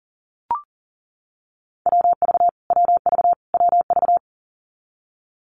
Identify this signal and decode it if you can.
Morse code, heard as a keyed tone, about 40 wpm, 700 Hz. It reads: WV WV WV
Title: US States Repeated 3x 40wpm